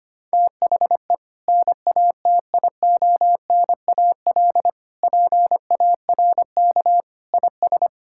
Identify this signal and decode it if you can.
Morse code, heard as a keyed tone, about 25 wpm, 700 Hz. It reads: THE NATIONAL PARK IS
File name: practice / 25wpm / Sentences with 2-Word Phrases - 25wpm